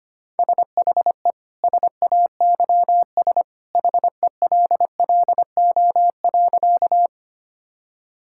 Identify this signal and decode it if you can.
Morse code, heard as a keyed tone, about 25 wpm, 700 Hz. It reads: SHE SAYS HELLO.
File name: practice / 25wpm / Sentences with 2-Word Phrases - 25wpm